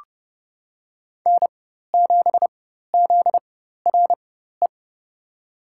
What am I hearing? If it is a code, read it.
Morse code, heard as a keyed tone, about 30 wpm, spaced as for 14 wpm, 700 Hz. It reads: N7ZRE
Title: USA Call Signs - 5 Characters - 4x Character Spacing - 30wpm